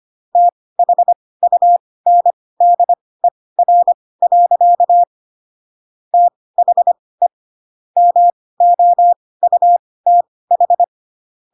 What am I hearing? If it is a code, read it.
Morse code, heard as a keyed tone, about 25 wpm, spaced as for 18 wpm, 700 Hz. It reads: THUNDER. THE MOUTH
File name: Pinocchio ch3 12 25 ik1hge